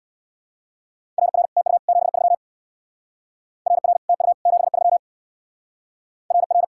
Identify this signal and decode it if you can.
Morse code, heard as a keyed tone, about 64 wpm, 700 Hz. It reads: RR TU 73 RR TU 73 RR